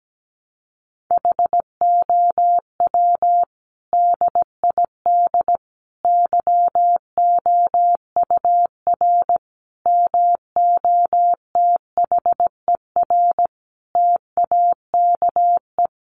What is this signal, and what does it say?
Morse code, heard as a keyed tone, about 17 wpm, 700 Hz. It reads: HOW DID YOUR MOTHER TAKE
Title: Sentences from Top 700 Words 17wpm